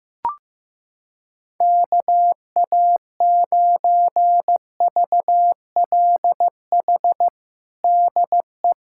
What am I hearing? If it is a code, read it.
Morse code, heard as a keyed tone, about 16 wpm, 700 Hz. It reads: KA9VLH DE
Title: USA Callsign DE Callsign - 6 Characters - 15wpm